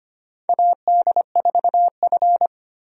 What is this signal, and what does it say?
Morse code, heard as a keyed tone, about 25 wpm, 700 Hz. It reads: AD4F
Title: USA Call Signs - 4 Characters - 25wpm